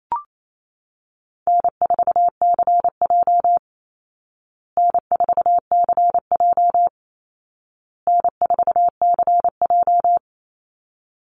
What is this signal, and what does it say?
Morse code, heard as a keyed tone, about 28 wpm, 700 Hz. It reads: N4CJ N4CJ N4CJ
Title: Easy Call Signs Repeated 3x - 28wpm